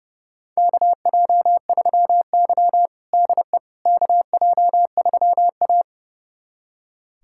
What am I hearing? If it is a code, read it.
Morse code, heard as a keyed tone, about 30 wpm, 700 Hz. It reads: KJ3Y DE KJ3A
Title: USA Callsign DE Callsign - 4 Characters - 30wpm